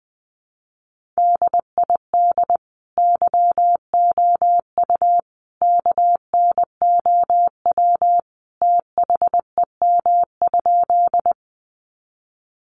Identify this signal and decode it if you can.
Morse code, heard as a keyed tone, about 20 wpm, 700 Hz. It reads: DID YOU KNOW THEM?
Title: Sentences from Top 100 Words 20wpm